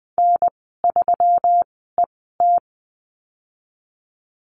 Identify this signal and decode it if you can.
Morse code, heard as a keyed tone, about 20 wpm, spaced as for 14 wpm, 700 Hz. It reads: N3ET
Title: 147 Lesson 24g - Letter y - Call Signs 20wpm